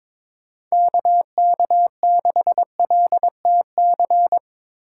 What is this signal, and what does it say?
Morse code, heard as a keyed tone, about 25 wpm, 700 Hz. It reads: KK6LTC
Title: Speed Racing - USA 2x3 Call Signs 22wpm